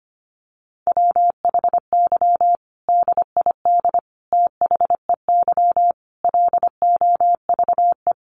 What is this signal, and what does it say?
Morse code, heard as a keyed tone, about 25 wpm, 700 Hz. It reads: WHY DID THEY LOVE
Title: Sentences from Top 1500 Words 25wpm